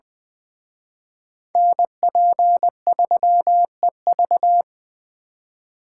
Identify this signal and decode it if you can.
Morse code, heard as a keyed tone, about 23 wpm, 700 Hz. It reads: NP3EV
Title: Speed Racing - USA 2x2 Call Signs 20wpm